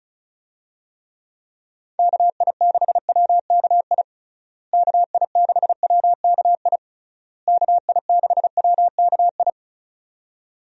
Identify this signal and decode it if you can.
Morse code, heard as a keyed tone, about 36 wpm, 700 Hz. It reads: KI6WKI KI6WKI KI6WKI
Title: USA Call Signs - 6 Characters - Repeated 3x - 35wpm